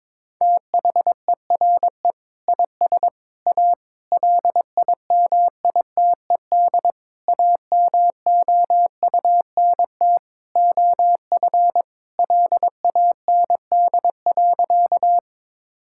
Morse code, heard as a keyed tone, about 22 wpm, 700 Hz. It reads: THERE IS A LIMITED AMOUNT OF LAND.
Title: Sentences from Top 1000 Words - Review Part 6 - 22wpm